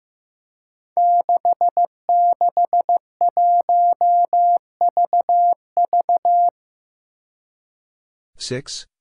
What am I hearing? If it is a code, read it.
Morse code, heard as a keyed tone, about 15 wpm, 700 Hz. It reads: 661VV
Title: Five Challenging Letter-Number Combinations 15wpm